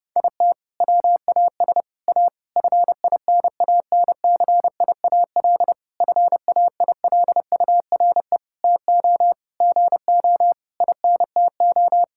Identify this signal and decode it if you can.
Morse code, heard as a keyed tone, about 30 wpm, 700 Hz. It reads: IT WAS A FINANCIAL FAILURE TO GO INTO
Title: Sentences from Top 1000 Words - Review Part 6 - 30wpm